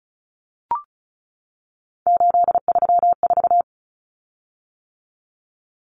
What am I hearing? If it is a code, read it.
Morse code, heard as a keyed tone, about 34 wpm, 700 Hz. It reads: 834
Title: Common Cut Numbers - 3-Digit 35wpm